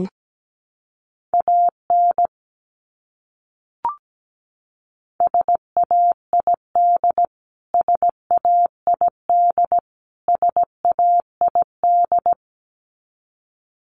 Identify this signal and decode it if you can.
Morse code, heard as a keyed tone, about 17 wpm, 700 Hz. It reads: AN SAID SAID SAID
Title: Top 500 Words Repeated 17wpm